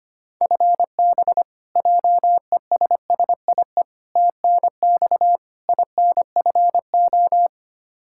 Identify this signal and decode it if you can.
Morse code, heard as a keyed tone, about 25 wpm, 700 Hz. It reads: FB JESSIE TNX INFO
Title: Practice QSO - Exchange 3 - More Info - 25wpm